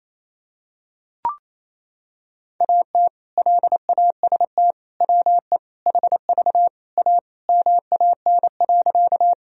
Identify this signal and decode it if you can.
Morse code, heard as a keyed tone, about 28 wpm, 700 Hz. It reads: AT LAST WE HV A MAN.
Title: Sentences from Top 300 Words and Abbreviations 28wpm